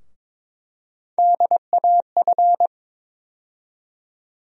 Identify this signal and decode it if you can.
Morse code, heard as a keyed tone, about 22 wpm, 700 Hz. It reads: DAF
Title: USA Call Sign Postfixes - 3 Letters 22wpm